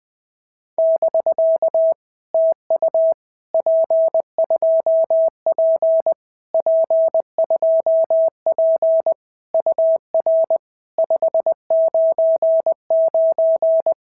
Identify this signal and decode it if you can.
Morse code, heard as a keyed tone, about 20 wpm, 650 Hz. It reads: <BK> TU P2P P2P UR 599
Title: POTA QSO Practice - 20wpm